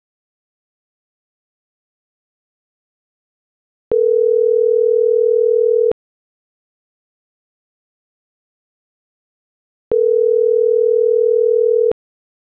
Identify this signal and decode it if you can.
Ringback tone (North America), 440 + 480 Hz.